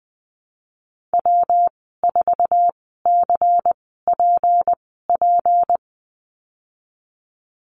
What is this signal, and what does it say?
Morse code, heard as a keyed tone, about 19 wpm, spaced as for 14 wpm, 700 Hz. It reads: W4CPP
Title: 161 Lesson 26g - Letter g - Call Signs 20wpm